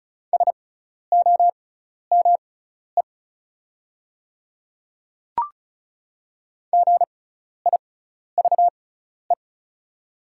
Morse code, heard as a keyed tone, about 36 wpm, spaced as for 12 wpm, 700 Hz. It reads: SOME GIVE
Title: 6x Character Spacing - Top 300 Words - 35wpm